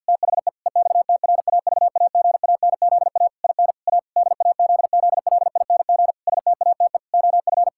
Morse code, heard as a keyed tone, about 50 wpm, 700 Hz. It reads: THE EXTRAVAGANZA IN A DAZZLING STATE OF
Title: Pangram Challenge v2 - 50wpm